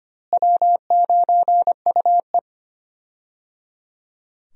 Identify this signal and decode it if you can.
Morse code, heard as a keyed tone, about 25 wpm, 700 Hz. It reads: W9UE
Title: USA Call Signs - 4 Characters - Repeated 3x - 25wpm